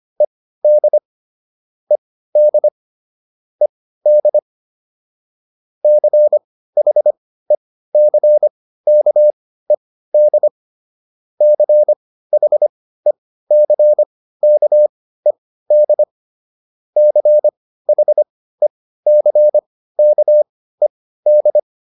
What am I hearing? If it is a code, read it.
Morse code, heard as a keyed tone, about 25 wpm, spaced as for 15 wpm, 600 Hz. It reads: ED ED ED CHECKED CHECKED CHECKED